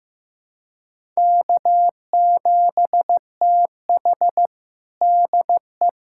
Morse code, heard as a keyed tone, about 15 wpm, 700 Hz. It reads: K7TH DE
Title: USA Callsign DE Callsign - 4 Characters - 15wpm